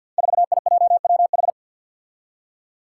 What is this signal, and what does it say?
Morse code, heard as a keyed tone, about 53 wpm, 700 Hz. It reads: 4IJWH
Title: Five Challenging Letter-Number Combinations 50wpm